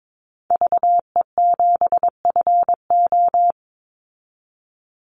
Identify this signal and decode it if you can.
Morse code, heard as a keyed tone, about 22 wpm, 700 Hz. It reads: VE7FO
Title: Call Signs Repeated 3x - 22wpm